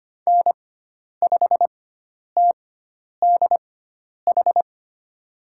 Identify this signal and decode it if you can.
Morse code, heard as a keyed tone, about 25 wpm, spaced as for 10 wpm, 700 Hz. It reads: N5TDH
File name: Lesson 16g - Letter c - Call Signs 25wpm